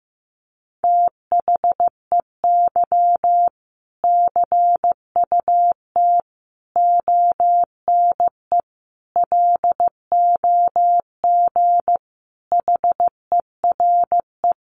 Morse code, heard as a keyed tone, about 15 wpm, 700 Hz. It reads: THEY CUT ONE LOG HERE
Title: Sentences from Top 1400 Words 15wpm